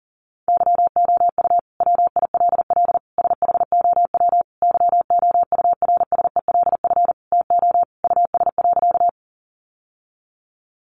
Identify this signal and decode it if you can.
Morse code, heard as a keyed tone, about 37 wpm, 700 Hz. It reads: YOU WILL SHOW YOURSELF TO US.
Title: Sentences from Top 600 Words 40wpm